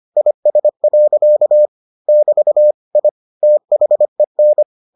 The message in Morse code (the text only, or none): IS. <BT> I THEN